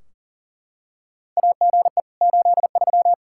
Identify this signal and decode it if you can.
Morse code, heard as a keyed tone, about 38 wpm, 700 Hz. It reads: AGE 83
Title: K1USN SST QSO Practice 40wpm